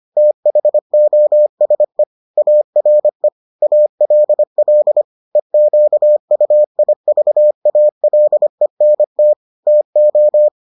Morse code, heard as a keyed tone, about 25 wpm, 600 Hz. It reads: THOSE ARE ALL EQUIVALENT TO